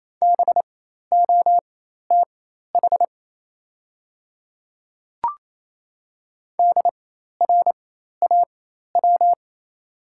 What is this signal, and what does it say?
Morse code, heard as a keyed tone, about 28 wpm, spaced as for 13 wpm, 700 Hz. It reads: BOTH DRAW
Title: 4x Character Spacing - Top 300 Words - 28wpm